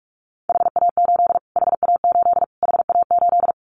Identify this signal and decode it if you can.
Morse code, heard as a keyed tone, about 46 wpm, 700 Hz. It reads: HA8 HA8 HA8
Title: Callsign Prefixes for Top 25 DXCC - Repeated 3x - 45wpm